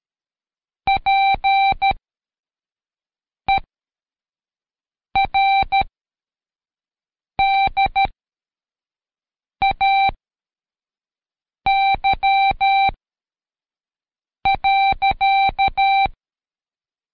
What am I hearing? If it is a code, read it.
Morse code, heard as a keyed tone, about 13 wpm, spaced as for 5 wpm, 780 Hz. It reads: PER DAY.
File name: Code Quick / AudioDownld2 / Seg102-Day12-FinalDrill